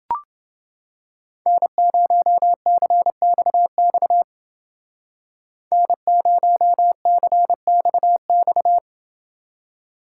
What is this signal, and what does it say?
Morse code, heard as a keyed tone, about 29 wpm, 700 Hz. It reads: N0CXX N0CXX